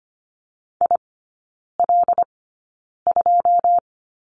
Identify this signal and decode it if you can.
Morse code, heard as a keyed tone, about 25 wpm, spaced as for 9 wpm, 700 Hz. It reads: IL2
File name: Lesson 14e - Number 2 - Three Characters 25wpm